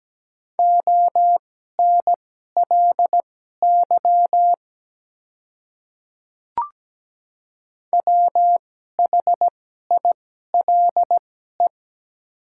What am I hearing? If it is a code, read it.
Morse code, heard as a keyed tone, about 17 wpm, spaced as for 12 wpm, 700 Hz. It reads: ONLY WHILE
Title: 2x Character Spacing - Top 300 Words - 17wpm